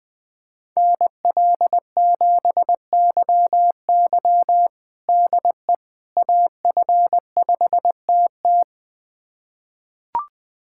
Morse code, heard as a keyed tone, about 20 wpm, 700 Hz. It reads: NL7YY DE AF5TT
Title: USA Callsign DE Callsign - 5 Characters - 20wpm